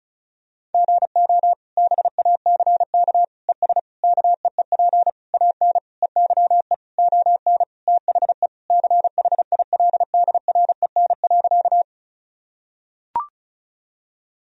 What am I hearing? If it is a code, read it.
Morse code, heard as a keyed tone, about 34 wpm, 700 Hz. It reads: GO BACK ES KEEP AN EYE ON THE CHILDREN.
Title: Sentences from Top 300 Words and Abbreviations 35wpm